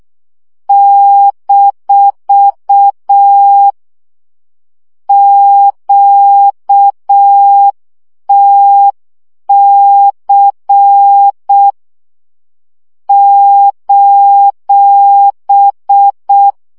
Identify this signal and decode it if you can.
Morse code, heard as a keyed tone, about 6 wpm, 800 Hz. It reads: - QTC :
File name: morze vizsga 2011-1 5